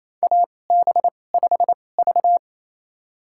Morse code, heard as a keyed tone, about 28 wpm, spaced as for 20 wpm, 700 Hz. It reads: AB5V
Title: USA Call Signs - 4 Characters - 2x Character Spacing - 28wpm